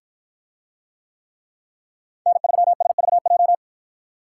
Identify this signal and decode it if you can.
Morse code, heard as a keyed tone, about 53 wpm, 700 Hz. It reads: N3SVJ